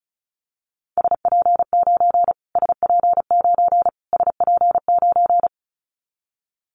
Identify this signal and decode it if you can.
Morse code, heard as a keyed tone, about 34 wpm, 700 Hz. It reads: SP9 SP9 SP9
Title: Callsign Prefixes for Top 25 DXCC - Repeated 3x - 35wpm